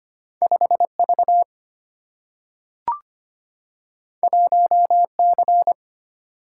Morse code, heard as a keyed tone, about 25 wpm, 700 Hz. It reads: 5V 1C
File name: Two Letter-Number Combinations - 25wpm